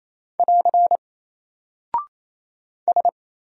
Morse code, heard as a keyed tone, about 27 wpm, 700 Hz. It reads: <AR> S
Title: Single Character (Letter:Number:Punctuation:Prosign) 28wpm